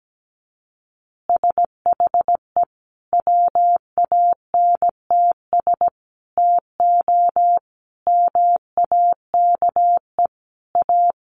Morse code, heard as a keyed tone, about 17 wpm, 700 Hz. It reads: SHE WANTS TO MAKE A
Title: Easy Phrases 17wpm